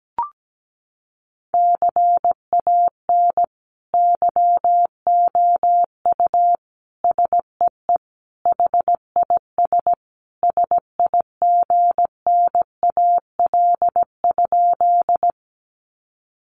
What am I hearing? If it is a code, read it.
Morse code, heard as a keyed tone, about 17 wpm, 700 Hz. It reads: CAN YOU SEE HIS SIGNAL?
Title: Sentences from Top 1400 Words 17wpm